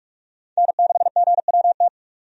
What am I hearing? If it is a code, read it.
Morse code, heard as a keyed tone, about 46 wpm, 700 Hz. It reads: N6GWT
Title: USA Call Signs - 5 Characters - 45wpm